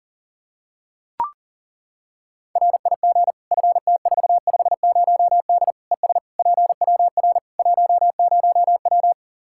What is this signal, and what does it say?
Morse code, heard as a keyed tone, about 40 wpm, 700 Hz. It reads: RIG FT450D ES PWR 10W
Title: Practice QSO - Exchange 2 - Info - 40wpm